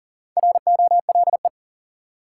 Morse code, heard as a keyed tone, about 40 wpm, 700 Hz. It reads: ROLE
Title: Sentences from Top 1000 Words - Review Part 6 - New Word Drill - 40wpm